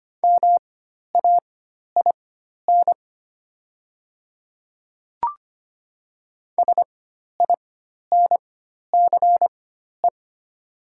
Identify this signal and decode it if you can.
Morse code, heard as a keyed tone, about 25 wpm, spaced as for 12 wpm, 700 Hz. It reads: MAIN SINCE
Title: 4x Character Spacing - Top 300 Words - 25wpm